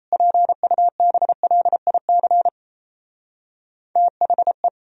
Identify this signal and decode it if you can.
Morse code, heard as a keyed tone, about 37 wpm, 700 Hz. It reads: PUBLIC THE